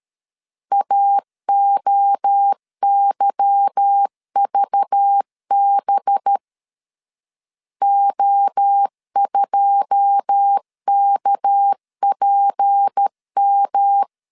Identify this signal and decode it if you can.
Morse code, heard as a keyed tone, about 13 wpm, 800 Hz. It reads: AOYVB O2KPM
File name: LEZIONI / 9 2 12wpm 2301